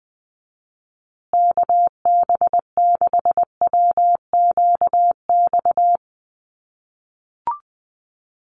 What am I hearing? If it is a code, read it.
Morse code, heard as a keyed tone, about 20 wpm, 700 Hz. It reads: KB6WQX
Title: USA Call Signs Repeated 3x 20wpm